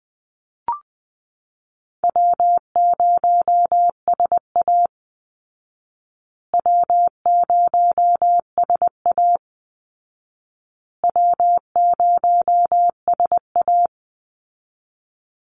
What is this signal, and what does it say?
Morse code, heard as a keyed tone, about 19 wpm, 700 Hz. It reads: W0SA W0SA W0SA
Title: Easy Call Signs Repeated 3x - 20wpm